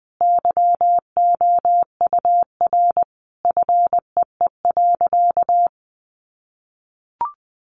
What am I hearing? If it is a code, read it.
Morse code, heard as a keyed tone, about 20 wpm, 700 Hz. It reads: YOUR FEE.
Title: A Sound Of Thunder 20wpm